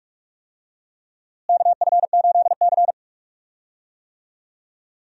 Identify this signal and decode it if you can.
Morse code, heard as a keyed tone, about 46 wpm, 700 Hz. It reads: KF8C
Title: USA Call Signs - 4 Characters - Repeated 3x - 45wpm